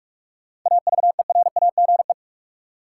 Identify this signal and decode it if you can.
Morse code, heard as a keyed tone, about 45 wpm, 700 Hz. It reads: AVERAGE
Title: Sentences from Top 1000 Words - Review Part 6 - New Word Drill - 45wpm